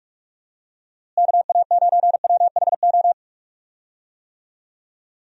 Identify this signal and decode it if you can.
Morse code, heard as a keyed tone, about 41 wpm, 700 Hz. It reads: KA9WHO
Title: USA Call Signs - 6 Characters - Repeated 2x - 45wpm